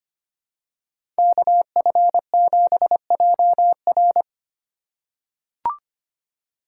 Morse code, heard as a keyed tone, about 25 wpm, 700 Hz. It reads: KF7JR
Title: USA Call Signs - 5 Characters - 3x Character Spacing - 25wpm